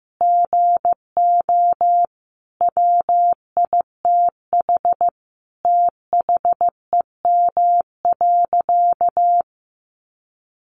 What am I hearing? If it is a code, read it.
Morse code, heard as a keyed tone, about 15 wpm, 700 Hz. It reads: GO WITH THEM.
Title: Sentences from Top 700 Words 15wpm